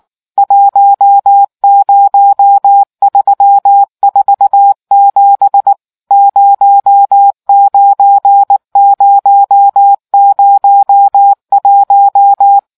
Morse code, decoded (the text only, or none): 10347 09001